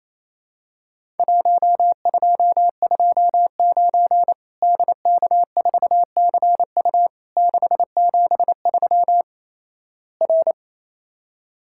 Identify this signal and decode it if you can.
Morse code, heard as a keyed tone, about 27 wpm, 700 Hz. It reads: 1229 DK4CU 673 R
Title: Worked All Europe DX Contest - QTC Exchange - 28wpm